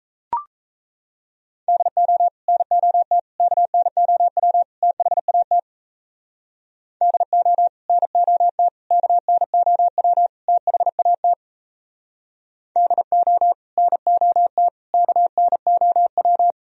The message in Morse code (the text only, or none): DO NOT KNOW THAT DO NOT KNOW THAT DO NOT KNOW